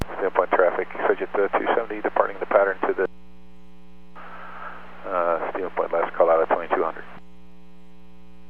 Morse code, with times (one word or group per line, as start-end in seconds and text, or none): none